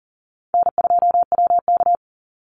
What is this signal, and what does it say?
Morse code, heard as a keyed tone, about 41 wpm, 700 Hz. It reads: N2WK
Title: Easy Call Signs - 40wpm